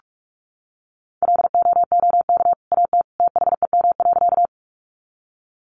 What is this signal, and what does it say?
Morse code, heard as a keyed tone, about 44 wpm, 700 Hz. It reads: LOOK AT THEM.
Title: Sentences from Top 100 Words 45wpm